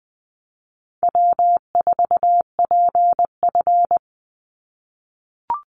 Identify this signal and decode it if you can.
Morse code, heard as a keyed tone, about 20 wpm, 700 Hz. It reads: W4PF
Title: Easy Call Signs - 20wpm